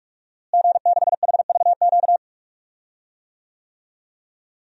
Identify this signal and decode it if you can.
Morse code, heard as a keyed tone, about 43 wpm, 700 Hz. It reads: G6HVQ
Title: DXCC-100 Call Signs - 5 Characters - Repeated 3x - 45wpm